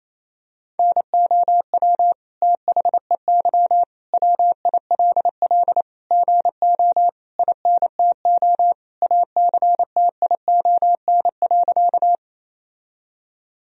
Morse code, heard as a keyed tone, about 28 wpm, 700 Hz. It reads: NOW THEY WILL GO INTO ACTION.
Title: Sentences from Top 1000 Words - Review Part 4 - 28wpm